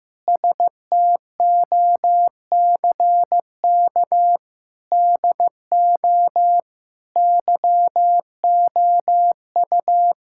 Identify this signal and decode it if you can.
Morse code, heard as a keyed tone, about 16 wpm, 700 Hz. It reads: STOCK DO YOU
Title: Sentences from Top 1000 Words - Review Part 5 - 15wpm